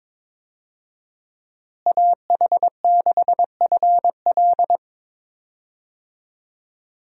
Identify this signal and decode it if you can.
Morse code, heard as a keyed tone, about 22 wpm, 700 Hz. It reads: AH6FL
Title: USA Call Signs - 5 Characters - Repeated 3x - 22wpm